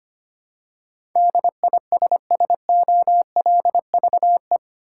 Morse code, heard as a keyed tone, about 25 wpm, 700 Hz. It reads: DISSOLVE
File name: Eight Letter Words - 25wpm